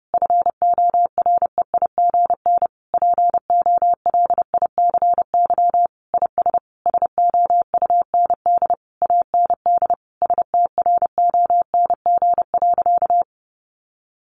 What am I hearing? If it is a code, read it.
Morse code, heard as a keyed tone, about 30 wpm, 700 Hz. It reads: FOREIGN POLICY IS SOUND AND STRONG.
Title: Sentences from Top 1400 Words 30wpm